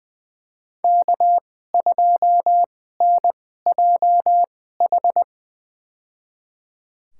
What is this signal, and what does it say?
Morse code, heard as a keyed tone, about 20 wpm, spaced as for 14 wpm, 700 Hz. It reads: K2NJH
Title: USA Call Signs - 2x Character Spacing - 20wpm